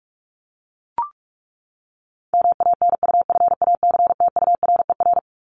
Morse code, heard as a keyed tone, about 44 wpm, 700 Hz. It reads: MANUFACTURER
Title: Top 1751-2000 Words 45wpm